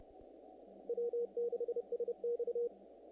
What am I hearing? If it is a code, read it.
Morse code, heard as a keyed tone, about 30 wpm, 480 Hz. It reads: W6SX
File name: contests / naqp-cw-k5tr-op-k5pi- -1900